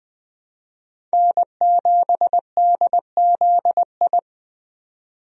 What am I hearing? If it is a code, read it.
Morse code, heard as a keyed tone, about 20 wpm, 700 Hz. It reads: N7DZI
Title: USA Call Signs - 5 Characters - 20wpm